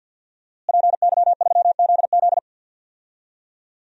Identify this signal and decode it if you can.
Morse code, heard as a keyed tone, about 53 wpm, 700 Hz. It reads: PY3ZZ
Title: DXCC-100 Call Signs - 5 Characters - Repeated 2x - 50wpm